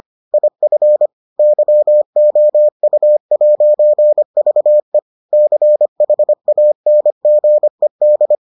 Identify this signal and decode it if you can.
Morse code, heard as a keyed tone, about 25 wpm, 600 Hz. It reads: IF YOU'VE CHANGED